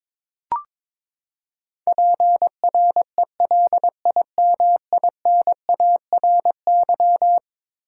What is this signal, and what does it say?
Morse code, heard as a keyed tone, about 22 wpm, 700 Hz. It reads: PRELIMINARY
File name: practice / 22wpm / Eleven Letter Words - 22wpm